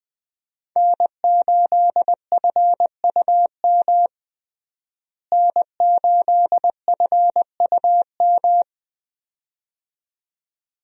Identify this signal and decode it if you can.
Morse code, heard as a keyed tone, about 20 wpm, 700 Hz. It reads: N8FUM N8FUM
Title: USA Call Signs - 5 Characters - Repeated 2x - 20wpm